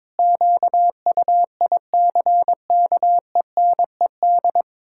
Morse code, heard as a keyed tone, about 21 wpm, 700 Hz. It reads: QUICKENED
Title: Pangram Challenge v2 - 22wpm